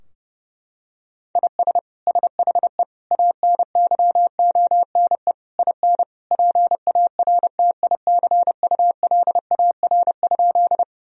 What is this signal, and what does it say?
Morse code, heard as a keyed tone, about 30 wpm, 700 Hz. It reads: IS SHE ANYONE IN PARTICULAR?
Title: Sentences from Top 1000 Words - Review Part 4 - 30wpm